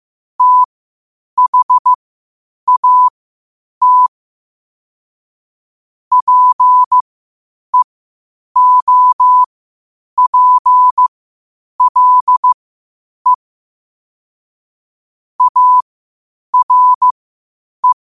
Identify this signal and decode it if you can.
Morse code, heard as a keyed tone, about 15 wpm, spaced as for 8 wpm, 1000 Hz. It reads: THAT PEOPLE ARE
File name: qotdfiles / 8wpm